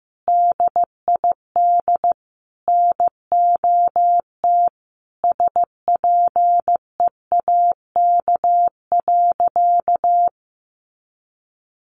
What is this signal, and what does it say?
Morse code, heard as a keyed tone, about 14 wpm, 700 Hz. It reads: DID NOT SPEAK.
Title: Sentences from Top 1100 Words 15wpm